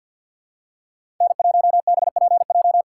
Morse code, heard as a keyed tone, about 52 wpm, 700 Hz. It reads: N1BPJ
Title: USA Call Signs - 4x Character Spacing - 50wpm